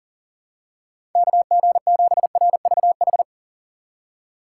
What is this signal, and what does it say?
Morse code, heard as a keyed tone, about 40 wpm, 700 Hz. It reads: KG7RVH
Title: USA Call Signs - 6 Characters - 4x Character Spacing - 40wpm